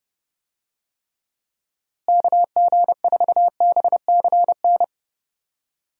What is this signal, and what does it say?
Morse code, heard as a keyed tone, about 32 wpm, 700 Hz. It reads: KG4BCN
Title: USA Call Signs - 6 Characters - Repeated 3x - 30wpm